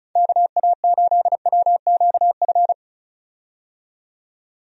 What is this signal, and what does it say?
Morse code, heard as a keyed tone, about 36 wpm, 700 Hz. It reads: KA8WQF
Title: USA Call Signs - 6 Characters - 35wpm